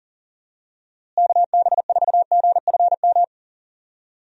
Morse code, heard as a keyed tone, about 40 wpm, 700 Hz. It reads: KB4GFM
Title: USA Call Signs - 6 Characters - Repeated 2x - 40wpm